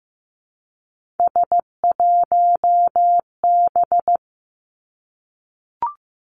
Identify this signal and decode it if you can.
Morse code, heard as a keyed tone, about 15 wpm, 700 Hz. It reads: S1B
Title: Three Challenging Letter-Number Combinations 15wpm